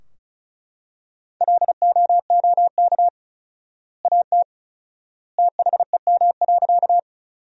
Morse code, heard as a keyed tone, about 36 wpm, 700 Hz. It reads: LOOK AT THEM.
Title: 4x Word-Spacing - Sentences from Top 100 Words - 35wpm